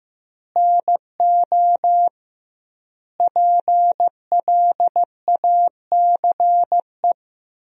Morse code, heard as a keyed tone, about 15 wpm, 700 Hz. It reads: NO PLACE
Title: 2x Word-Spacing - Sentences from Top 200 Words - 15wpm